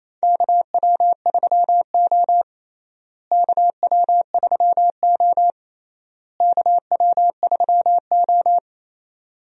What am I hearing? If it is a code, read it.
Morse code, heard as a keyed tone, about 28 wpm, 700 Hz. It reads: KW3O KW3O KW3O
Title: USA Call Signs - 4 Characters - Repeated 3x - 28wpm